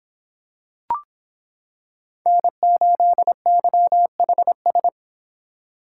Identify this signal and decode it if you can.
Morse code, heard as a keyed tone, about 25 wpm, 700 Hz. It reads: N8YHS